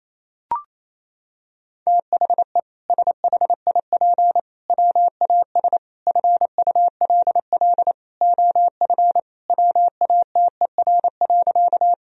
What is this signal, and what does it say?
Morse code, heard as a keyed tone, about 27 wpm, 700 Hz. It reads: THE SHIP WAS FULL OF WATER.
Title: Sentences from Top 1000 Words - Review Part 3 - 28wpm